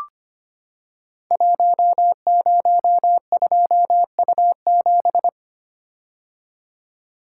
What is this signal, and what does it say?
Morse code, heard as a keyed tone, about 25 wpm, 700 Hz. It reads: 102U7
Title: Code Groups - 5 Letter:Numbers - 25wpm